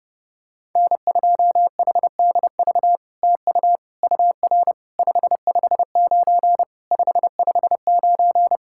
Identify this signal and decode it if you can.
Morse code, heard as a keyed tone, about 32 wpm, 700 Hz. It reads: N2HDV TU UR 559 559
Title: SOTA QSO Practice 30wpm